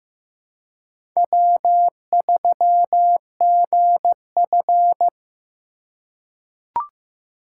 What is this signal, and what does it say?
Morse code, heard as a keyed tone, about 15 wpm, 700 Hz. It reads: W3GF
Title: USA Call Signs - 4 Characters - 2x Character Spacing - 15wpm